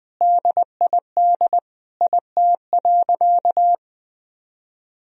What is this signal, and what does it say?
Morse code, heard as a keyed tone, about 20 wpm, 700 Hz. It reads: DID IT.
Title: Sentences from Top 1000 Words - Review Part 4 - 20wpm